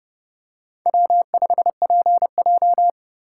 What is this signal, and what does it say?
Morse code, heard as a keyed tone, about 29 wpm, 700 Hz. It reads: W5PJ